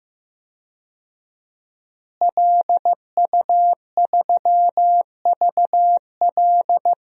Morse code, heard as a keyed tone, about 15 wpm, 700 Hz. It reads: LU3VL
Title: DXCC-100 Call Signs - 5 Characters - Repeated 2x - 15wpm